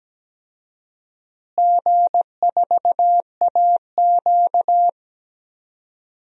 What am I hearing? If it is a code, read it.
Morse code, heard as a keyed tone, about 17 wpm, 700 Hz. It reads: G4AQ
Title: DXCC-100 Call Signs - 4 Characters - Repeated 2x - 17wpm